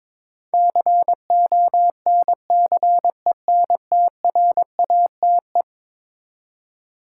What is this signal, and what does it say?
Morse code, heard as a keyed tone, about 22 wpm, 700 Hz. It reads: CONCENTRATE
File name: practice / 22wpm / Eleven Letter Words - 22wpm